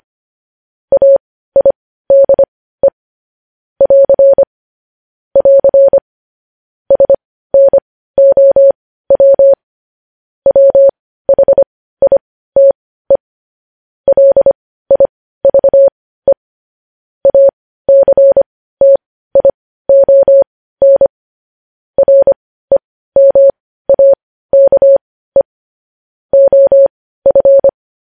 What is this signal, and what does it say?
Morse code, heard as a keyed tone, about 25 wpm, spaced as for 15 wpm, 550 Hz. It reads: AIDE <AR> <AR> SNOW WHITE LIVE ACTION REMAKE OF
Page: news-headlines-in-morse-code-at-15-wpm